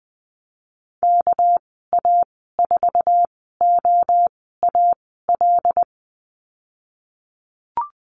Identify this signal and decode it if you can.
Morse code, heard as a keyed tone, about 20 wpm, spaced as for 15 wpm, 700 Hz. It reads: KA4OAL
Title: 207 Lesson 33g - Letter j - Call Signs 20wpm